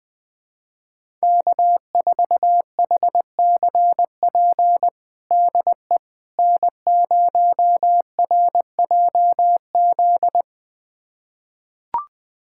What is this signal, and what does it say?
Morse code, heard as a keyed tone, about 20 wpm, 700 Hz. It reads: K4HCP DE N0RJZ
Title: USA Callsign DE Callsign - 5 Characters - 20wpm